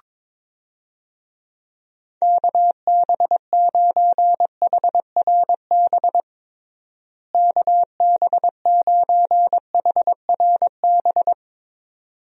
Morse code, heard as a keyed tone, about 22 wpm, 700 Hz. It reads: KB9HRB KB9HRB
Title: USA Call Signs - 6 Characters - Repeated 2x - 22wpm